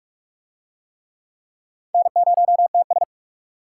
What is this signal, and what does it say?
Morse code, heard as a keyed tone, about 44 wpm, 700 Hz. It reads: N0TS